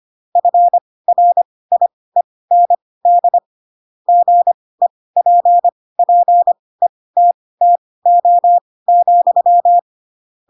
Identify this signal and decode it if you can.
Morse code, heard as a keyed tone, about 25 wpm, spaced as for 18 wpm, 700 Hz. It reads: FRIEND GEPPETTO,
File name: Pinocchio / Pinocchio ch2 12 25 ik1hge